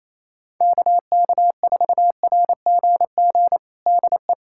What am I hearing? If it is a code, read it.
Morse code, heard as a keyed tone, about 28 wpm, 700 Hz. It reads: KK4RGG DE
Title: USA Callsign DE Callsign 28wpm